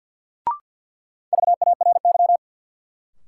Morse code, heard as a keyed tone, about 48 wpm, 700 Hz. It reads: VARY
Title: Sentences from Top 1200 Words - New Word Drill - 50wpm